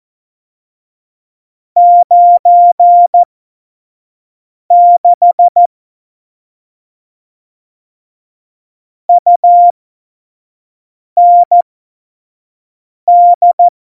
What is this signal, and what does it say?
Morse code, heard as a keyed tone, about 14 wpm, spaced as for 5 wpm, 700 Hz. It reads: 96 UND
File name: Morsekurs / Witze / Witz 18. Februar 2020, 25 F70